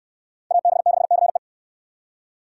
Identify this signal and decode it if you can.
Morse code, heard as a keyed tone, about 69 wpm, 690 Hz. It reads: AB6CE